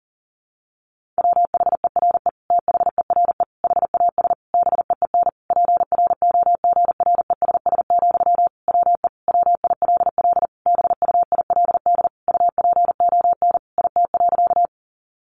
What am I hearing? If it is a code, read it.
Morse code, heard as a keyed tone, about 40 wpm, 700 Hz. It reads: WHERE THERE HAS BEEN PROGRESS, WE WILL BUILD UPON IT.
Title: Sentences from Top 1400 Words 40wpm